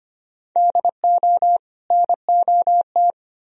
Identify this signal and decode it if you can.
Morse code, heard as a keyed tone, about 33 wpm, 700 Hz. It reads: DO NOT